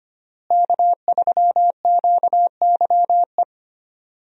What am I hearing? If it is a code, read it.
Morse code, heard as a keyed tone, about 25 wpm, 700 Hz. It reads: K3QYE
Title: USA Call Signs - 5 Characters - Repeated 3x - 25wpm